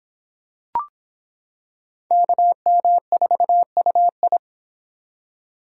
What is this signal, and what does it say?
Morse code, heard as a keyed tone, about 26 wpm, 700 Hz. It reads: KM4UI